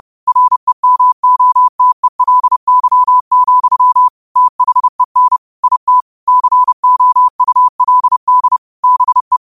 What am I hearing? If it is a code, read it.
Morse code, heard as a keyed tone, about 30 wpm, 1000 Hz. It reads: REMOTELY, THEN IT COULD BE